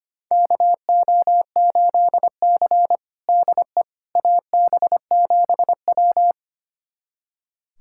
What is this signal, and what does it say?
Morse code, heard as a keyed tone, about 25 wpm, 700 Hz. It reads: KO8C DE AB7W
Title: USA Callsign DE Callsign - 4 Characters - 25wpm